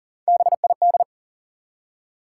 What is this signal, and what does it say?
Morse code, heard as a keyed tone, about 37 wpm, 700 Hz. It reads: BID
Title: Three Letter Words - 40wpm